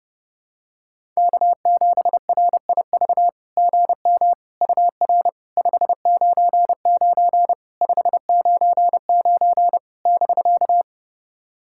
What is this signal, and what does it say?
Morse code, heard as a keyed tone, about 32 wpm, 700 Hz. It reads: K7RIV GM UR 599 599 <BK>
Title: SOTA QSO Practice 30wpm